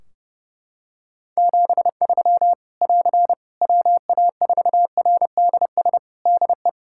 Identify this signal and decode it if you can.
Morse code, heard as a keyed tone, about 30 wpm, 700 Hz. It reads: 73 <AR> WA4RDS DE
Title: Practice QSO - Exchange 4 - Ending - 30wpm